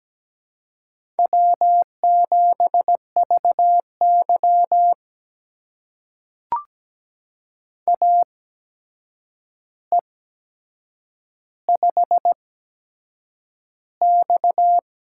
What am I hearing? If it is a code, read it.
Morse code, heard as a keyed tone, about 17 wpm, spaced as for 7 wpm, 700 Hz. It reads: W7VY AE5X
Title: USA Call Signs - 4 Characters - 8x Character Spacing - 17wpm